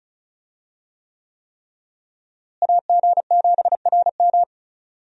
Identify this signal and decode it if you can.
Morse code, heard as a keyed tone, about 36 wpm, 700 Hz. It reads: AG7RM
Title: USA Call Signs - 5 Characters - Repeated 3x - 35wpm